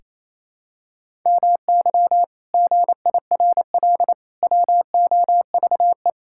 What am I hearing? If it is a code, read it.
Morse code, heard as a keyed tone, about 28 wpm, 700 Hz. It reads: MY GIRL WOVE
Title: Pangram Challenge v2 - 28wpm